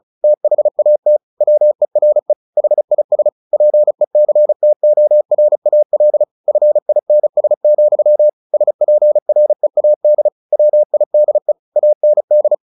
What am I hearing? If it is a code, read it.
Morse code, heard as a keyed tone, about 34 wpm, 600 Hz. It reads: THAT WERE HIS PECTORAL FINS, SPREAD WIDE AND
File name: the old man and the sea / the old man and the sea 1 35wpm